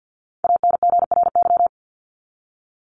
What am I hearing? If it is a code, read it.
Morse code, heard as a keyed tone, about 50 wpm, 700 Hz. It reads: ANGRY
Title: Top 1751-2000 Words 50wpm